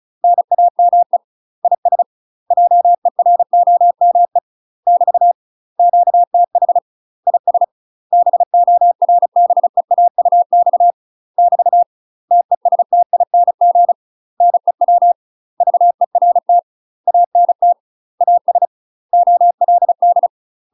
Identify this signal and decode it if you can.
Morse code, heard as a keyed tone, about 34 wpm, 700 Hz. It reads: NAME IS JEROME <BT> QTH IS BORBEAUX <BT> TESTING NEW VERT ANT AS OLD
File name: UK MASTER QSOs TRAINING / 35WPM / UK MASTER QSOs 0004